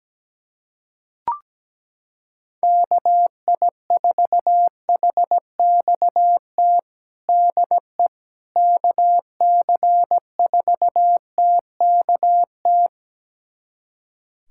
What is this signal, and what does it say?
Morse code, heard as a keyed tone, about 17 wpm, 700 Hz. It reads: KI4HXT DE KC4TKT
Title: USA Callsign DE Callsign - 6 Characters - 17wpm